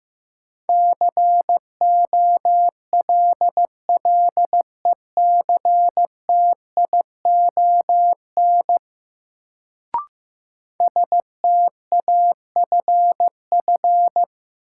Morse code, heard as a keyed tone, about 15 wpm, 700 Hz. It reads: COLLECTION STAFF
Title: Sentences from Top 1200 Words - New Word Drill - 15wpm